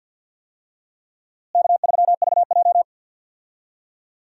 Morse code, heard as a keyed tone, about 49 wpm, 700 Hz. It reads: K3VJ
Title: USA Call Signs - 4 Characters - Repeated 3x - 50wpm